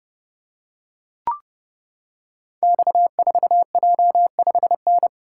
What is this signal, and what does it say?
Morse code, heard as a keyed tone, about 30 wpm, 700 Hz. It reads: X4J5N
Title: Code Groups - 5 Letter:Numbers - 30wpm